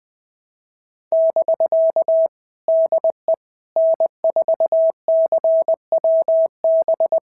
Morse code, heard as a keyed tone, about 20 wpm, 650 Hz. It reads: <BK> DE N4CWB
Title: POTA QSO Practice - 20wpm